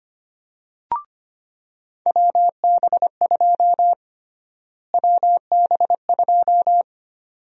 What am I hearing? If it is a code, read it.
Morse code, heard as a keyed tone, about 25 wpm, 700 Hz. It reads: WB2 WB2
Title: USA Call Sign Prefixes - Repeated 2x - 25wpm